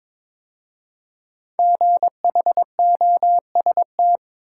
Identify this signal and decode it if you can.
Morse code, heard as a keyed tone, about 22 wpm, 700 Hz. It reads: GHOST
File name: 5 Letter Words - First 3 Letters Given - 22wpm